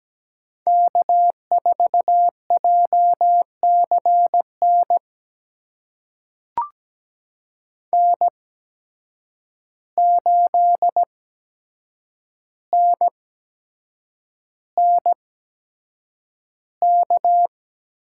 Morse code, heard as a keyed tone, about 17 wpm, spaced as for 7 wpm, 700 Hz. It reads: K4JCN N8NNK
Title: USA Call Signs - 8x Character Spacing - 17wpm